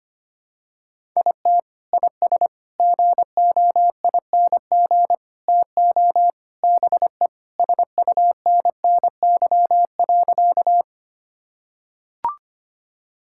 Morse code, heard as a keyed tone, about 25 wpm, 700 Hz. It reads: IT IS GOING TO BE SUNNY.
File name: practice / 25wpm / Sentences with 4-Word Phrases - 25wpm